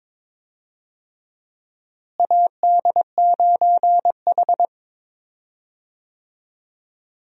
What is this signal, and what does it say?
Morse code, heard as a keyed tone, about 22 wpm, 700 Hz. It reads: AD9H
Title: USA Call Signs - 4 Characters - Repeated 3x - 22wpm